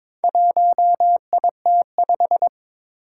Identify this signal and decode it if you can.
Morse code, heard as a keyed tone, about 22 wpm, 700 Hz. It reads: 1IT5
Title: Four Challenging Letter-Number Combinations 22wpm